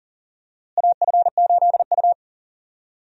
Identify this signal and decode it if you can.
Morse code, heard as a keyed tone, about 40 wpm, 700 Hz. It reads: AF8U
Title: USA Call Signs - 4 Characters - Repeated 2x - 40wpm